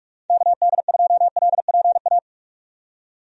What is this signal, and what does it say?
Morse code, heard as a keyed tone, about 41 wpm, 700 Hz. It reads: KD2LPA
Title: USA Call Signs - 6 Characters - 45wpm